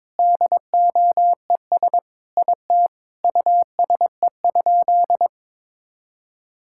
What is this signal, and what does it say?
Morse code, heard as a keyed tone, about 22 wpm, 700 Hz. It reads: DOES IT USE?
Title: Sentences from Top 1000 Words - Review Part 6 - 22wpm